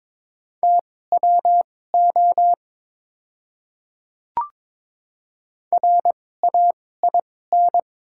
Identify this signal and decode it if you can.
Morse code, heard as a keyed tone, about 22 wpm, spaced as for 16 wpm, 700 Hz. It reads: TWO RAIN
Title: 2x Character Spacing - Top 300 Words - 22wpm